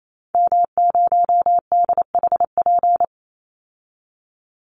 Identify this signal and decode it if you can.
Morse code, heard as a keyed tone, about 28 wpm, 700 Hz. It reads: M0DHP
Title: Call Signs Repeated 2x - 28wpm